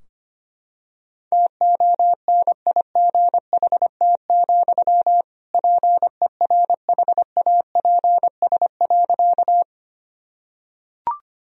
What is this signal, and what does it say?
Morse code, heard as a keyed tone, about 25 wpm, 700 Hz. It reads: TONIGHT, PERHAPS.
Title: Sentences from Top 1000 Words - Review Part 3 - 25wpm